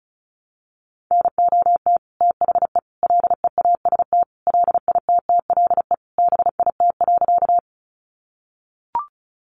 Morse code, heard as a keyed tone, about 34 wpm, 700 Hz. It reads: NOT THE LEAST LITTLE BIT.
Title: Sentences from Top 900 Words 35wpm